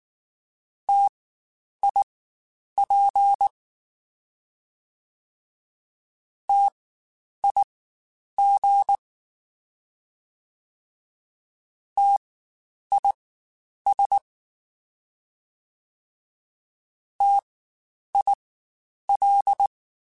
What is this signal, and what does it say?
Morse code, heard as a keyed tone, about 19 wpm, spaced as for 9 wpm, 800 Hz. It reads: TIP TIG TIS TIL